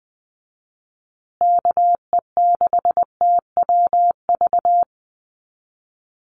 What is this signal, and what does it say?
Morse code, heard as a keyed tone, about 20 wpm, 700 Hz. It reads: KE6TWV
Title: USA Call Signs Repeated 2x 20wpm